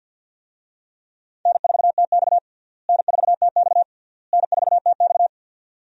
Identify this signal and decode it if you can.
Morse code, heard as a keyed tone, about 53 wpm, 700 Hz. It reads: N4TX N4TX N4TX
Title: USA Call Signs - 4 Characters - Repeated 3x - 50wpm